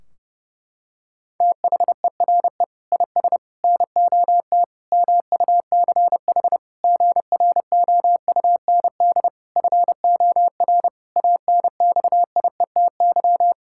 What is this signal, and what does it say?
Morse code, heard as a keyed tone, about 30 wpm, 700 Hz. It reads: THERE IS NOT MUCH GROUND FOR ANXIETY
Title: Sentences from Top 1600 Words 30wpm